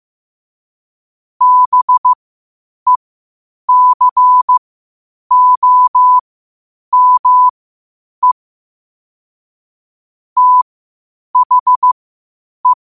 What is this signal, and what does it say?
Morse code, heard as a keyed tone, about 15 wpm, spaced as for 8 wpm, 1000 Hz. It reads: BECOME THE
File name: 8wpm